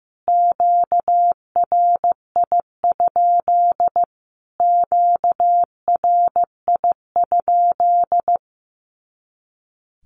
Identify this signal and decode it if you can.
Morse code, heard as a keyed tone, about 15 wpm, 700 Hz. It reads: QRI? QRI?
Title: Q-Codes Repeated 3x 15wpm